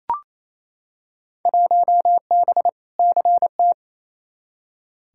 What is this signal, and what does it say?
Morse code, heard as a keyed tone, about 28 wpm, 700 Hz. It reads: 1B CT
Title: Field Day Exchanges Encore - 28wpm